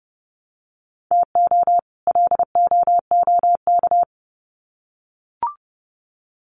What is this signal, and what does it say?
Morse code, heard as a keyed tone, about 30 wpm, 700 Hz. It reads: TO LOOK
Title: Sets of 2 Words 30wpm